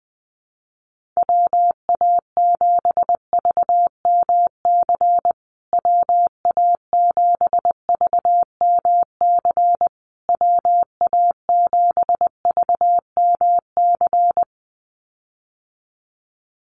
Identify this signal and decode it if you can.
Morse code, heard as a keyed tone, about 20 wpm, 700 Hz. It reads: WA7VMC WA7VMC WA7VMC
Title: USA Call Signs Repeated 3x 20wpm